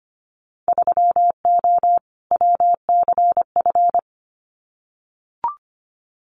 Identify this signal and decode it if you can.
Morse code, heard as a keyed tone, about 25 wpm, 700 Hz. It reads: 3O WCF
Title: Winter Field Day Exchanges 25wpm